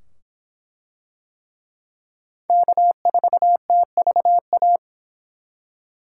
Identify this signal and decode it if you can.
Morse code, heard as a keyed tone, about 24 wpm, 700 Hz. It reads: K4TVA